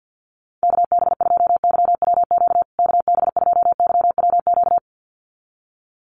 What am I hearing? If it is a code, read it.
Morse code, heard as a keyed tone, about 50 wpm, 700 Hz. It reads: KB2YWQ KB2YWQ
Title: USA Call Signs Repeated 2x 50wpm